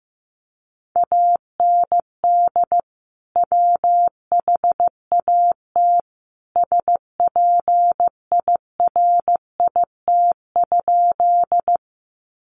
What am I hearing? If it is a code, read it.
Morse code, heard as a keyed tone, about 15 wpm, 700 Hz. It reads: AND WHAT SPIRIT?
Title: Sentences from Top 1400 Words 15wpm